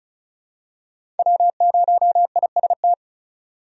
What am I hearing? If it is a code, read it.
Morse code, heard as a keyed tone, about 36 wpm, 700 Hz. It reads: W0IST
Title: USA Call Signs - 5 Characters - 35wpm